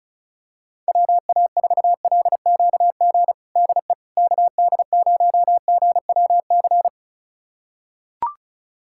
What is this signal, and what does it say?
Morse code, heard as a keyed tone, about 36 wpm, 700 Hz. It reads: WA4LQG DE KD0GWC
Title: USA Callsign DE Callsign - 6 Characters - 35wpm